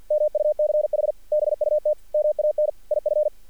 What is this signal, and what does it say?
Morse code, heard as a keyed tone, about 48 wpm, 600 Hz. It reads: MUCH BUT MAN IF
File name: archived / CRSnet